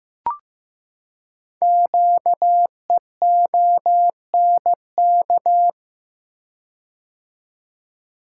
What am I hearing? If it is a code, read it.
Morse code, heard as a keyed tone, about 15 wpm, 700 Hz. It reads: QEONK
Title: Code Groups - 5 Letters - 15wpm